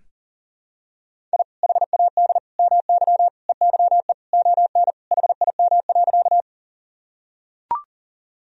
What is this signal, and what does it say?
Morse code, heard as a keyed tone, about 40 wpm, 700 Hz. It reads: I HAD MY EYE ON HIM.
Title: Sentences from Top 1000 Words - Review Part 1 - 40wpm